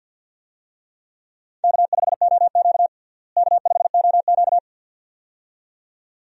Morse code, heard as a keyed tone, about 52 wpm, 700 Hz. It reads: K5OQ K5OQ
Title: USA Call Signs - 4 Characters - Repeated 2x - 50wpm